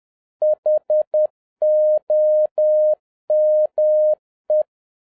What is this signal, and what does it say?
Morse code, heard as a keyed tone, about 10 wpm, 610 Hz. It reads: HOME
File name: 2011-07-mp3 / n31 THE 3.11 2 10wpm